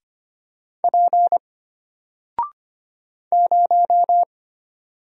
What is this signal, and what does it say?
Morse code, heard as a keyed tone, about 25 wpm, 700 Hz. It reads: P 0
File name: Single-Letter-Number-25wpm